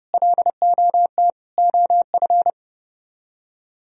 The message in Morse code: LOT OF

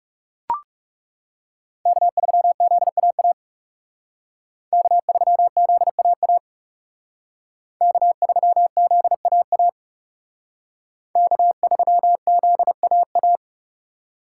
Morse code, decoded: K3ZAA K3ZAA K3ZAA K3ZAA

